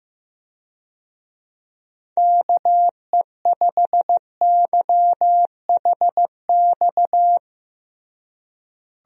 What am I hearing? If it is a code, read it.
Morse code, heard as a keyed tone, about 15 wpm, 700 Hz. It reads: KE5YHX
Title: USA Call Signs - 6 Characters - Repeated 3x - 15wpm